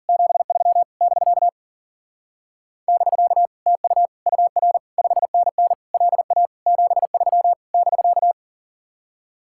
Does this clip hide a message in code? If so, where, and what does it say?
Yes: Morse code from 0.1 to 8.3 s, heard as a keyed tone, about 52 wpm, 700 Hz. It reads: 73 <BK> <BK> TU UR 5NN LA 73 <BK>